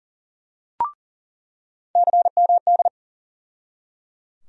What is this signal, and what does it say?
Morse code, heard as a keyed tone, about 37 wpm, 700 Hz. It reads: CMD
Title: USA Call Sign Postfixes - 3 Letters 40wpm